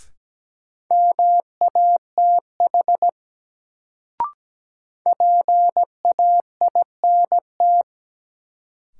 Morse code, heard as a keyed tone, about 17 wpm, 700 Hz. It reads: MATH PAINT
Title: Sentences from Top 1000 Words - Review Part 6 - New Word Drill - 17wpm